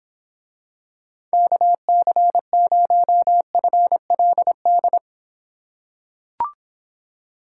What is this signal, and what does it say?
Morse code, heard as a keyed tone, about 26 wpm, 700 Hz. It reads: KC0FLD